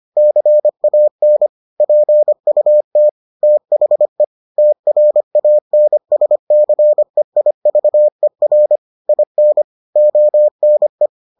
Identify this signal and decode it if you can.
Morse code, heard as a keyed tone, about 25 wpm, 600 Hz. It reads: CAN PUT THE TRANSCEIVER IN ONE